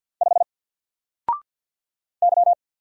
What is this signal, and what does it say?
Morse code, heard as a keyed tone, about 50 wpm, 700 Hz. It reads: 5 Y
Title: Single Character (Letter:Number:Punctuation:Prosign) 50wpm